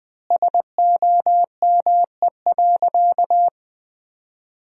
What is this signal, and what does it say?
Morse code, heard as a keyed tone, about 20 wpm, 700 Hz. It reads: SOME.
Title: 2x Word-Spacing - Sentences from Top 100 Words - 20wpm